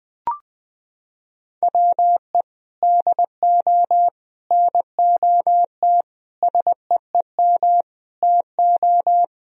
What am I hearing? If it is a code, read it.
Morse code, heard as a keyed tone, about 21 wpm, 700 Hz. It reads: WE DO NOT SEEM TO
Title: Sentences from Top 1000 Words - Review Part 1 - 20wpm